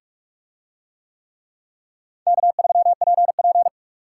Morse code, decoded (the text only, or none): K3PP